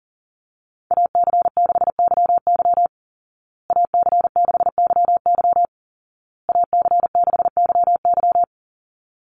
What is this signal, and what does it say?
Morse code, heard as a keyed tone, about 40 wpm, 700 Hz. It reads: AC6YY AC6YY AC6YY
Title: Call Signs Repeated 2x - 40wpm